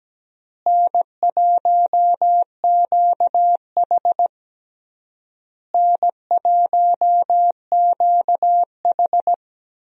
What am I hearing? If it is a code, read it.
Morse code, heard as a keyed tone, about 17 wpm, 700 Hz. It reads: N1QH N1QH
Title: USA Call Signs - 4 Characters - Repeated 3x - 17wpm